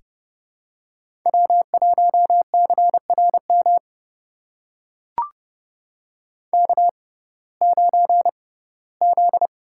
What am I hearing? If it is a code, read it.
Morse code, heard as a keyed tone, about 32 wpm, spaced as for 19 wpm, 700 Hz. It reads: W1CRM K9Z
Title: USA Call Signs - 6x Character Spacing - 30wpm